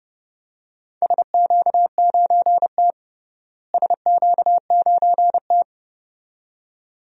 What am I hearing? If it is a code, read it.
Morse code, heard as a keyed tone, about 32 wpm, 700 Hz. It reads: SQ9T SQ9T
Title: DXCC-100 Call Signs - 4 Characters - Repeated 2x - 30wpm